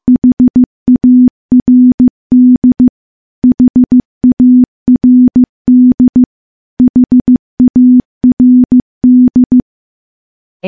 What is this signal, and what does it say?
Morse code, heard as a keyed tone, about 15 wpm, 250 Hz. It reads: HARD HARD HARD
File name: morse-training-15WPM